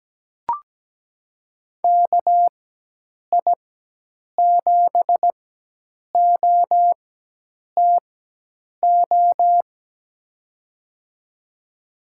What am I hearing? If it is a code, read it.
Morse code, heard as a keyed tone, about 17 wpm, spaced as for 8 wpm, 700 Hz. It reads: KI7OTO
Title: USA Call Signs - 6 Characters - 4x Character Spacing - 17wpm